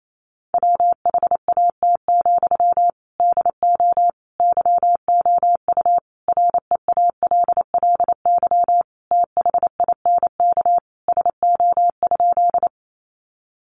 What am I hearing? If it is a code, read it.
Morse code, heard as a keyed tone, about 28 wpm, 700 Hz. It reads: WHAT, DO YOU REALLY THINK SO?
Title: Sentences from Top 600 Words 28wpm